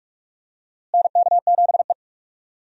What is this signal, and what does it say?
Morse code, heard as a keyed tone, about 46 wpm, 700 Hz. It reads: NK7E
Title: USA Call Signs - 4 Characters - Repeated 2x - 45wpm